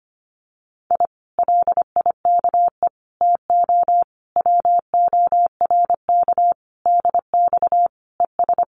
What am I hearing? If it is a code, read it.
Morse code, heard as a keyed tone, about 25 wpm, 700 Hz. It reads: I LIKE TO WORK DX ES
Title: Common QSO Phrases Encore 25wpm